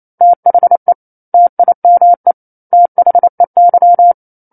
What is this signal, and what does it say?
Morse code, heard as a keyed tone, about 28 wpm, 700 Hz. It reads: THE TIME THEY
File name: 09-Chapter-09 28 WPM